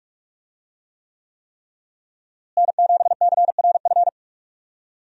Morse code, heard as a keyed tone, about 41 wpm, 700 Hz. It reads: N7CRF
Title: USA Call Signs - 5 Characters - Repeated 3x - 45wpm